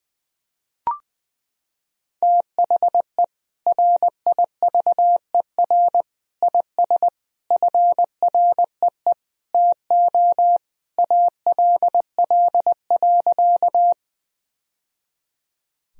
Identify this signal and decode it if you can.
Morse code, heard as a keyed tone, about 20 wpm, 700 Hz. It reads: THE RIVER IS FREE TO ALL.
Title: Sentences from Top 1000 Words - Review Part 1 - 20wpm